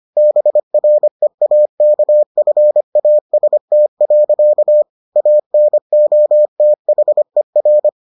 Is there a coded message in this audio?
Yes: Morse code from 0.2 to 7.9 s, heard as a keyed tone, about 25 wpm, 600 Hz. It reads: BREAKFAST. ANOTHER